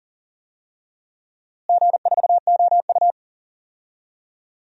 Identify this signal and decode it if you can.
Morse code, heard as a keyed tone, about 37 wpm, 700 Hz. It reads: G4OU
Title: DXCC-100 Call Signs - 4 Characters - Repeated 2x - 40wpm